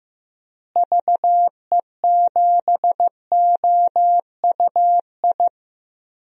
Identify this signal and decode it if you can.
Morse code, heard as a keyed tone, about 15 wpm, 700 Hz. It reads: VE7OUI
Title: DXCC-100 Call Signs - 6 Characters - Repeated 3x - 15wpm